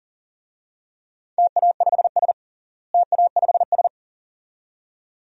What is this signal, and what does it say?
Morse code, heard as a keyed tone, about 40 wpm, 700 Hz. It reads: TA5S TA5S
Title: DXCC-100 Call Signs - 4 Characters - Repeated 2x - 40wpm